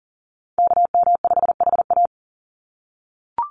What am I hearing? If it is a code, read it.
Morse code, heard as a keyed tone, about 40 wpm, 700 Hz. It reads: KM5HA
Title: USA Call Signs 40wpm